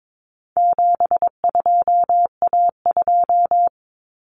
Morse code, heard as a keyed tone, about 22 wpm, 700 Hz. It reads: 72A2
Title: Common Cut Numbers - 4-Digit 22wpm